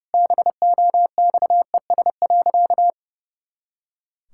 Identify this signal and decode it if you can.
Morse code, heard as a keyed tone, about 30 wpm, 700 Hz. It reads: BOXES.
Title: Pangram Challenge v2 - 30wpm